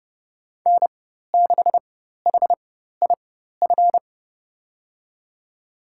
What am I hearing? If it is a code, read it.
Morse code, heard as a keyed tone, about 30 wpm, spaced as for 14 wpm, 700 Hz. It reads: N6HIF
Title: USA Call Signs - 5 Characters - 4x Character Spacing - 30wpm